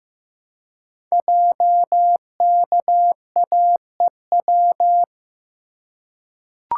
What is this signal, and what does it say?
Morse code, heard as a keyed tone, about 15 wpm, 700 Hz. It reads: JKAEW
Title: Code Groups - 5 Letters - 15wpm